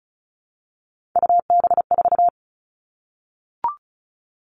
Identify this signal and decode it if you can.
Morse code, heard as a keyed tone, about 36 wpm, 700 Hz. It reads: UB4
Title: Three Challenging Letter-Number Combinations 35wpm